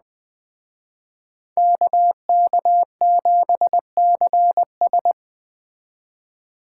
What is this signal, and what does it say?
Morse code, heard as a keyed tone, about 21 wpm, 700 Hz. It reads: KK7CS
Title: Speed Racing - USA Call Signs 17wpm